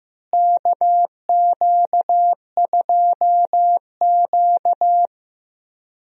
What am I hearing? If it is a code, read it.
Morse code, heard as a keyed tone, about 15 wpm, 700 Hz. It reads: KQ2Q
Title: USA Call Signs - 4 Characters - Repeated 2x - 15wpm